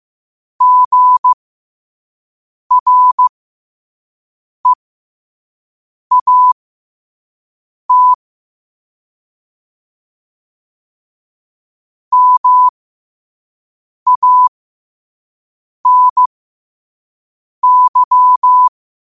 Morse code, heard as a keyed tone, about 15 wpm, spaced as for 5 wpm, 1000 Hz. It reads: GREAT MANY
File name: qotdfiles / 5wpm